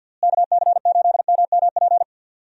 Morse code, heard as a keyed tone, about 50 wpm, 700 Hz. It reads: KC8MMP
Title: USA Call Signs - 6 Characters - Repeated 3x - 50wpm